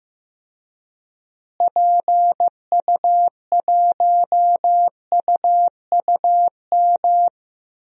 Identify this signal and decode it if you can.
Morse code, heard as a keyed tone, about 15 wpm, 700 Hz. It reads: PU1UUM
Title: DXCC-100 Call Signs - 6 Characters - Repeated 2x - 15wpm